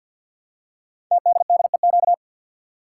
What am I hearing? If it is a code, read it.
Morse code, heard as a keyed tone, about 50 wpm, 700 Hz. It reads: TDDEQ
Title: Code Groups - 5 Letters - 50wpm